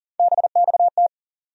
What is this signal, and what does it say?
Morse code, heard as a keyed tone, about 40 wpm, 700 Hz. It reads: BXT
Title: USA Call Sign Postfixes - 3 Letters 40wpm